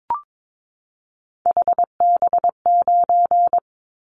Morse code, heard as a keyed tone, about 22 wpm, 700 Hz. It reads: HB9
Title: Callsign Prefixes for Top 25 DXCC - 22wpm